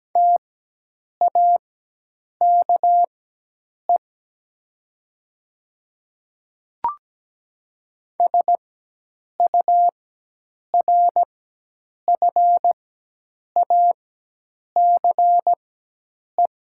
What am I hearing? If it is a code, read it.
Morse code, heard as a keyed tone, about 17 wpm, spaced as for 8 wpm, 700 Hz. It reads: TAKE SURFACE
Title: 4x Character Spacing - Top 500 Words - 17wpm